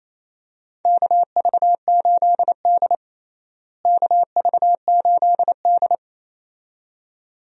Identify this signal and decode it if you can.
Morse code, heard as a keyed tone, about 28 wpm, 700 Hz. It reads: KV8D KV8D
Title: USA Call Signs - 4 Characters - Repeated 2x - 28wpm